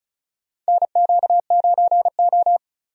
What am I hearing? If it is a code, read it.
Morse code, heard as a keyed tone, about 36 wpm, 700 Hz. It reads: NQ9O
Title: USA Call Signs - 4 Characters - 35wpm